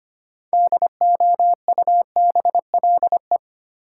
Morse code, heard as a keyed tone, about 25 wpm, 700 Hz. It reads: DOUBLE
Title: Sentences from Top 1400 Words - New Word Drill Alt - 25wpm